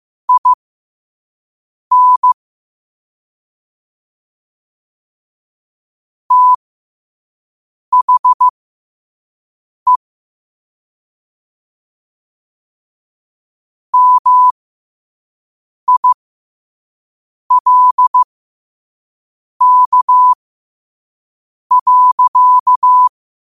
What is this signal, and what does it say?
Morse code, heard as a keyed tone, about 15 wpm, spaced as for 5 wpm, 1000 Hz. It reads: IN THE MILK.
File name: qotdfiles / 5wpm